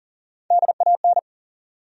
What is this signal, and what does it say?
Morse code, heard as a keyed tone, about 40 wpm, 700 Hz. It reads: DAN
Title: USA Common Names - Max 5 Letters - 40wpm